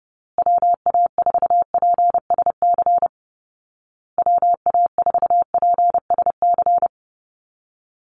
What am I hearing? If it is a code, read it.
Morse code, heard as a keyed tone, about 30 wpm, 700 Hz. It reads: WA4PSC WA4PSC
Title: Call Signs Repeated 2x - 30wpm